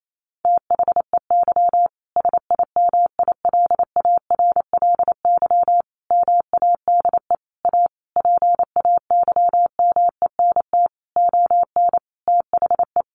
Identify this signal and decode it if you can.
Morse code, heard as a keyed tone, about 28 wpm, 700 Hz. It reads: THEY SIMILARLY MADE A PAYMENT ON THE
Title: Sentences from Top 1400 Words 28wpm